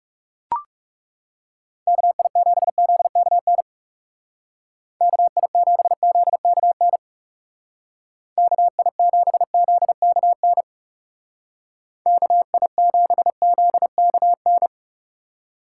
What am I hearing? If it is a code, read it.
Morse code, heard as a keyed tone, about 36 wpm, 700 Hz. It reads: KI7ZKN KI7ZKN KI7ZKN KI7ZKN
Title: Speed Racing - USA Call Signs 30wpm